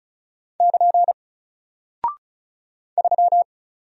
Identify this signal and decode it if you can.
Morse code, heard as a keyed tone, about 34 wpm, 700 Hz. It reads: <KN> 3
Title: Single Character (Letter:Number:Punctuation:Prosign) Alt 35wpm